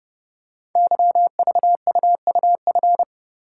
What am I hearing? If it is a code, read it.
Morse code, heard as a keyed tone, about 30 wpm, 700 Hz. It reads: YVUUF
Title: Code Groups - 5 Letter:Numbers - 30wpm